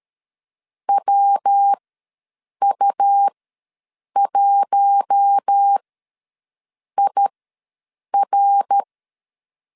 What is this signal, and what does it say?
Morse code, heard as a keyed tone, about 13 wpm, spaced as for 7 wpm, 800 Hz. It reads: WU1IR
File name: LEZIONI / 8 YJ45 / 8 Y ws 12wpm 3409